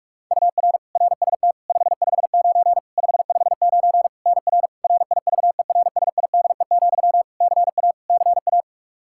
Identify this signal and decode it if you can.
Morse code, heard as a keyed tone, about 41 wpm, 700 Hz. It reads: UR RST 559 559 NR RIVERSIDE, CA CA
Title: Practice QSO - Exchange 1 - Beginning - 45wpm